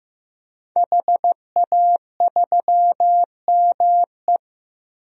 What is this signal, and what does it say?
Morse code, heard as a keyed tone, about 16 wpm, 700 Hz. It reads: HA3ME
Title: DXCC-100 Call Signs - 5 Characters - 15wpm